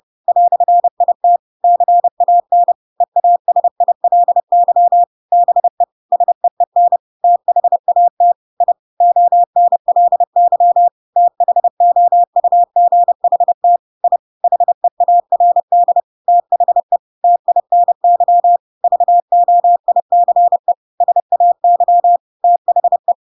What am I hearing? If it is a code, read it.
Morse code, heard as a keyed tone, about 30 wpm, 700 Hz. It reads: "IT CAN EASILY BE SEEN THAT I ONLY THOUGHT I HEARD THE TINY VOICE SAY THE
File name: Pinocchio / Pinocchio ch1 30 30 ik1hge